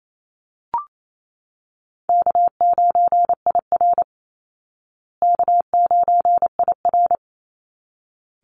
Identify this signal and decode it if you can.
Morse code, heard as a keyed tone, about 28 wpm, 700 Hz. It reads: K9IR K9IR
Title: Call Signs Repeated 3x - 28wpm